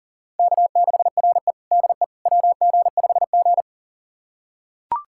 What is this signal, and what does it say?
Morse code, heard as a keyed tone, about 40 wpm, 700 Hz. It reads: K6RE DE WG5G
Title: USA Callsign DE Callsign - 4 Characters - 40wpm